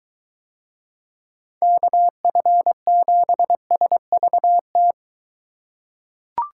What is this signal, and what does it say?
Morse code, heard as a keyed tone, about 21 wpm, 700 Hz. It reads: KF7SVT